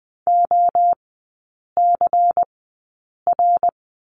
Morse code, heard as a keyed tone, about 20 wpm, spaced as for 8 wpm, 700 Hz. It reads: OCR
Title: 133 Lesson 22e - Question Mark - Three Characters 20wpm